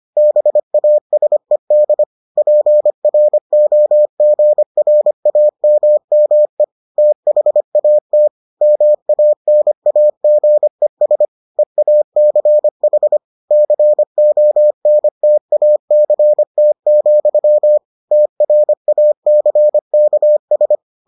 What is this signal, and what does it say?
Morse code, heard as a keyed tone, about 25 wpm, 600 Hz. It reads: BASED PROGRAMME THAT MANAGES EACH CONTACT, TRACKS